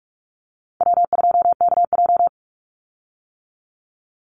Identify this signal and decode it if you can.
Morse code, heard as a keyed tone, about 44 wpm, 700 Hz. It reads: W2KJ
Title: Call Signs - 45wpm